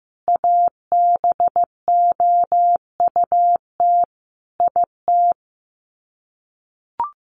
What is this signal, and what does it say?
Morse code, heard as a keyed tone, about 14 wpm, 700 Hz. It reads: ABOUT IT
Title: Sets of 5 Words Encore2 15wpm